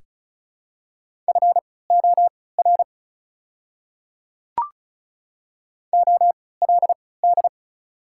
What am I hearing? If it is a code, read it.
Morse code, heard as a keyed tone, about 34 wpm, spaced as for 20 wpm, 700 Hz. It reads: FOR OLD
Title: 3x Character Spacing - Top 200 Words - 35wpm